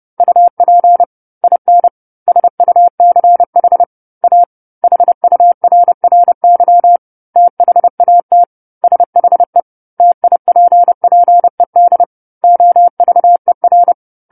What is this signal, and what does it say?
Morse code, heard as a keyed tone, about 30 wpm, 700 Hz. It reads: UP IN SUCH A HURRY THAT SHE TIPPED OVER
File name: 10-Chapter-10 30 WPM